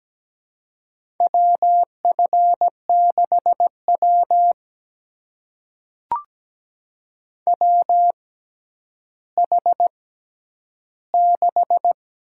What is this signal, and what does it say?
Morse code, heard as a keyed tone, about 17 wpm, spaced as for 10 wpm, 700 Hz. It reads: WF6W WH6
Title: USA Call Signs - 4 Characters - 6x Character Spacing - 17wpm